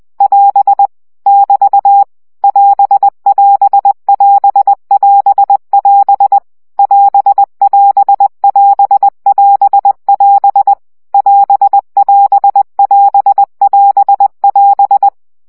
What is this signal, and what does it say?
Morse code, heard as a keyed tone, about 20 wpm, 800 Hz. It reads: <AS> <BT> <AS><AS><AS><AS><AS> <AS><AS><AS><AS><AS> <AS><AS><AS><AS><AS>
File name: CW gyakorlas as 17 baud